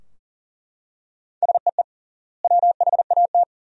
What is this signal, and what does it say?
Morse code, heard as a keyed tone, about 40 wpm, 700 Hz. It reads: SEE WHAT
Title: 3x Word-Spacing - Sentences from Top 500 Words - 40wpm